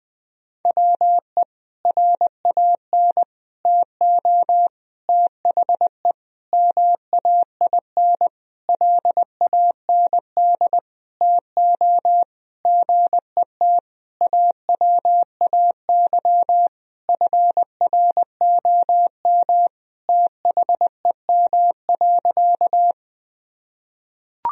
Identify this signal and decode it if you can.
Morse code, heard as a keyed tone, about 20 wpm, 700 Hz. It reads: WE RAN TO THE MAIN LAND TO GET AWAY FROM THEM.
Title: Sentences from Top 1000 Words - Review Part 2 - 20wpm